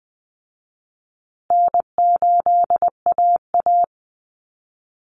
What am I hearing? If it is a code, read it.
Morse code, heard as a keyed tone, about 20 wpm, 700 Hz. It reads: N8AA
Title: Call Signs Repeated 3x - 20wpm